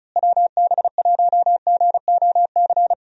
Morse code, heard as a keyed tone, about 36 wpm, 700 Hz. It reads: WB1GOC
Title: USA Call Signs - 6 Characters - Repeated 3x - 35wpm